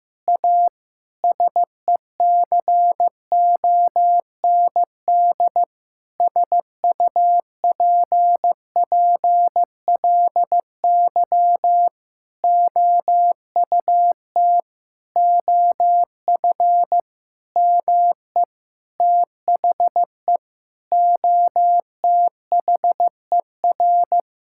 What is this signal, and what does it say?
Morse code, heard as a keyed tone, about 15 wpm, 700 Hz. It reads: A SECOND SUPPLY OUT OF ME THE OTHER
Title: Sentences from Top 1000 Words - Review Part 5 - 15wpm